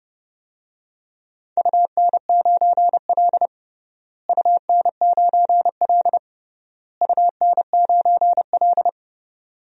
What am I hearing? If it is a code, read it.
Morse code, heard as a keyed tone, about 30 wpm, 700 Hz. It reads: UN9L UN9L UN9L
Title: DXCC-100 Call Signs - 4 Characters - Repeated 3x - 30wpm